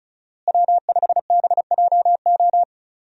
Morse code, heard as a keyed tone, about 36 wpm, 700 Hz. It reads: W5BJO
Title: USA Call Signs - 5 Characters - Repeated 3x - 35wpm